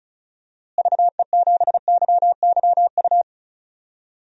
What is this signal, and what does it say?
Morse code, heard as a keyed tone, about 36 wpm, 700 Hz. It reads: VE7YYU
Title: DXCC-100 Call Signs - 6 Characters - Repeated 2x - 35wpm